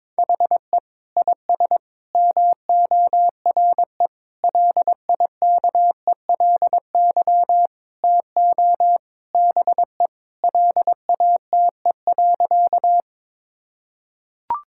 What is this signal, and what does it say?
Morse code, heard as a keyed tone, about 21 wpm, 700 Hz. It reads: HE IS MORE LIKELY TO BE LATE.
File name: practice / 22wpm / Sentences with 4-Word Phrases - 22wpm